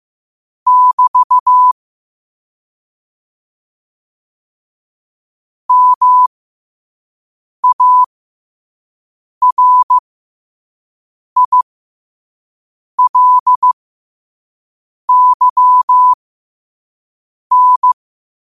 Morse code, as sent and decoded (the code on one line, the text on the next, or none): -...- / -- .- .-. .. .-.. -.-- -.
<BT> MARILYN